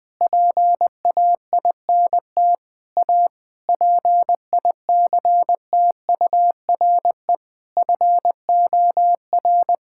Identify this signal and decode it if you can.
Morse code, heard as a keyed tone, about 21 wpm, 700 Hz. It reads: PAINT A PICTURE FOR
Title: Sentences from Top 1000 Words - Review Part 6 - 20wpm